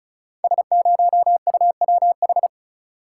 Morse code, heard as a keyed tone, about 36 wpm, 690 Hz. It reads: S0UWH
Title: Code Groups - 5 Letter:Numbers - 35wpm